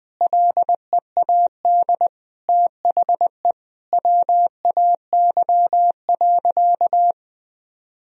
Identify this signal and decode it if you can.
Morse code, heard as a keyed tone, about 20 wpm, 700 Hz. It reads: LEAD THE WAY.
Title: Sentences from Top 1000 Words - Review Part 2 - 20wpm